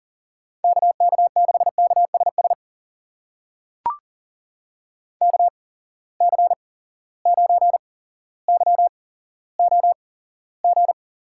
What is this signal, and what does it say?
Morse code, heard as a keyed tone, about 40 wpm, spaced as for 17 wpm, 700 Hz. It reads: KK6KSS KC9YOG
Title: USA Call Signs - 6 Characters - 8x Character Spacing - 40wpm